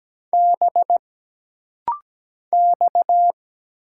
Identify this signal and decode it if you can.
Morse code, heard as a keyed tone, about 17 wpm, 700 Hz. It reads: B X
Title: Single Letter-Number Alt - Rapid-Fire 17wpm